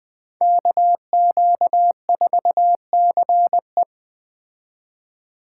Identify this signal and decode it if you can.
Morse code, heard as a keyed tone, about 20 wpm, 700 Hz. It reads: KQ4CE